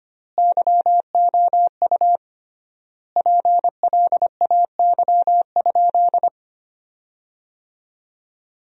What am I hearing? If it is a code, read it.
Morse code, heard as a keyed tone, about 25 wpm, 700 Hz. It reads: YOU PLAY?
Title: 3x Word-Spacing - Sentences from Top 300 Words - 25wpm